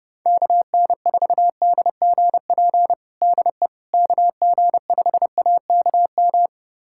Morse code, heard as a keyed tone, about 30 wpm, 700 Hz. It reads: KN4DGP DE KG5AKM
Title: USA Callsign DE Callsign - 6 Characters - 30wpm